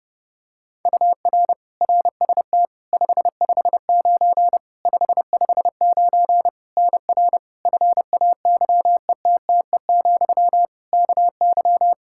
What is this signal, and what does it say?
Morse code, heard as a keyed tone, about 30 wpm, 700 Hz. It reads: UR RST 559 559 NR FAYETTE, KY
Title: Practice QSO - Exchange 1 - Beginning - 30wpm